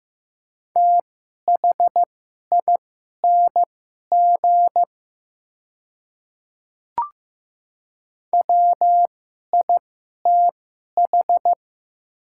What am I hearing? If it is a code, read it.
Morse code, heard as a keyed tone, about 15 wpm, spaced as for 11 wpm, 700 Hz. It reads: THING WITH
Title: 2x Character Spacing - Top 100 Words - 15wpm